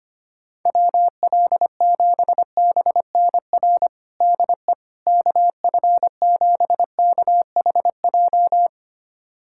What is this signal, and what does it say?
Morse code, heard as a keyed tone, about 25 wpm, 700 Hz. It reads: WL7BNR DE KF7KHJ
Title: USA Callsign DE Callsign - 6 Characters - 25wpm